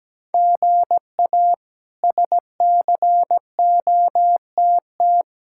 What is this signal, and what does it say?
Morse code, heard as a keyed tone, about 17 wpm, 700 Hz. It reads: GA SCOTT
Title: K1USN SST QSO Practice - 17wpm